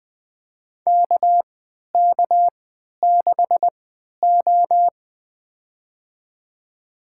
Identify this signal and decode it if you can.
Morse code, heard as a keyed tone, about 20 wpm, spaced as for 11 wpm, 700 Hz. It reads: KK6O
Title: USA Call Signs - 4 Characters - 3x Character Spacing - 20wpm